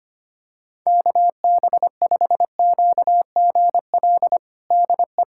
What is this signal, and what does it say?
Morse code, heard as a keyed tone, about 25 wpm, 700 Hz. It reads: KB5QGL DE
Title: USA Callsign DE Callsign - 6 Characters - 25wpm